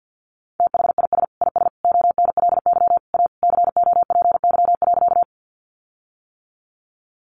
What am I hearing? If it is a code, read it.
Morse code, heard as a keyed tone, about 50 wpm, 700 Hz. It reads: THIS IS ONLY A COPY.
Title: Sentences from Top 1400 Words 50wpm